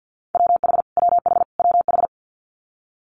Morse code, heard as a keyed tone, about 50 wpm, 690 Hz. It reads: WH WH WH
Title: Multi-letter Phonemes Repeated 3x 50wpm